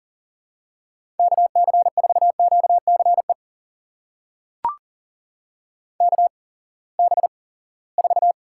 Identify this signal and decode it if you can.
Morse code, heard as a keyed tone, about 40 wpm, spaced as for 23 wpm, 700 Hz. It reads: KC4QCE KB4
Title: USA Call Signs - 6 Characters - 8x Character Spacing - 40wpm